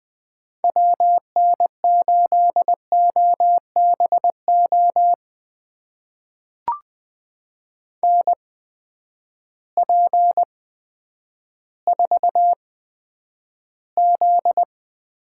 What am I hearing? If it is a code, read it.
Morse code, heard as a keyed tone, about 20 wpm, spaced as for 10 wpm, 700 Hz. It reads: WN8OBO NP4Z
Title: USA Call Signs - 8x Character Spacing - 20wpm